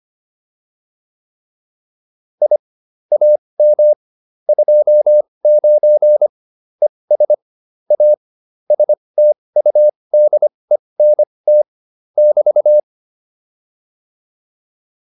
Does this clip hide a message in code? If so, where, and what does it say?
Yes: Morse code from 2.4 to 12.8 s, heard as a keyed tone, about 25 wpm, spaced as for 20 wpm, 600 Hz. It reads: I AM 29 ES A STUDENT <BT>